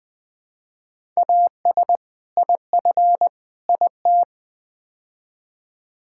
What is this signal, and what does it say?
Morse code, heard as a keyed tone, about 20 wpm, 700 Hz. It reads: AS IF IT
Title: Build A Phrase - 5-Word Phrases - 20wpm